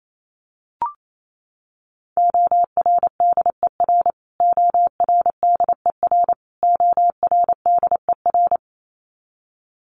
Morse code, heard as a keyed tone, about 28 wpm, 700 Hz. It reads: ORDER ORDER ORDER
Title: Top 500 Words Repeated 28wpm